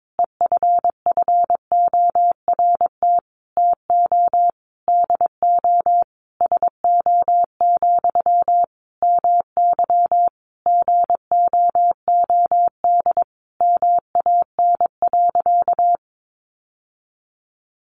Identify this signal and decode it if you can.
Morse code, heard as a keyed tone, about 22 wpm, 700 Hz. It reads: EFFORT TO DO SO, MY GOOD MAN.
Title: Sentences from Top 1200 Words 22wpm